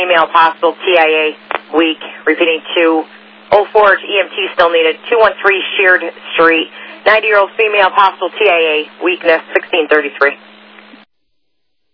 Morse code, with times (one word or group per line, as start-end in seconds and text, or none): none